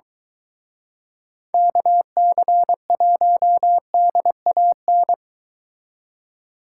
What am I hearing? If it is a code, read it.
Morse code, heard as a keyed tone, about 23 wpm, 700 Hz. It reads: KC1DAN